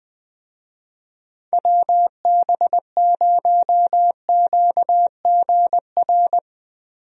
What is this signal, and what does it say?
Morse code, heard as a keyed tone, about 20 wpm, 700 Hz. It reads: WB0QGR
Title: USA Call Signs - 6 Characters - 20wpm